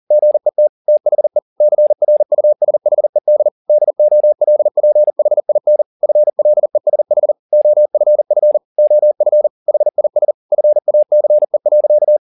Morse code, heard as a keyed tone, about 40 wpm, 600 Hz. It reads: GET THE CRUSHED DOLPHIN FLESH OFF OF HIS FACE.
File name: the old man and the sea 2 40wpm